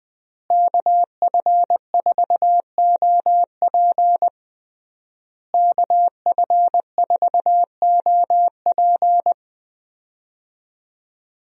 Morse code, heard as a keyed tone, about 20 wpm, 700 Hz. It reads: KF4OP KF4OP
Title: USA Call Signs - 5 Characters - Repeated 3x - 20wpm